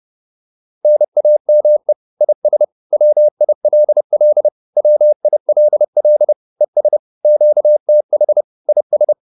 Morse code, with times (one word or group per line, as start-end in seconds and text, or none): 0.8-1.9 NAME
2.2-2.6 IS
2.9-4.5 WILL
4.8-6.3 WILL
6.6-7.0 ES
7.2-8.4 QTH
8.7-9.1 IS